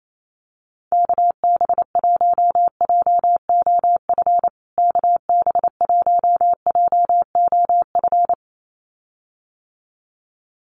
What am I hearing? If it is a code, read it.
Morse code, heard as a keyed tone, about 28 wpm, 700 Hz. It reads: KB1JOF KB1JOF
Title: USA Call Signs Repeated 2x 28wpm